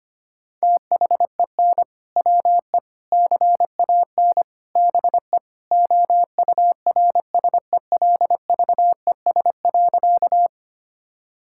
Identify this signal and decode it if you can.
Morse code, heard as a keyed tone, about 25 wpm, 700 Hz. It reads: THEN WE CAN BE OURSELVES.
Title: Sentences from Top 1600 Words 25wpm